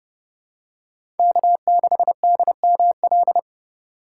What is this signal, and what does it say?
Morse code, heard as a keyed tone, about 32 wpm, 700 Hz. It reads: K6DML
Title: USA Call Signs - 4x Character Spacing - 30wpm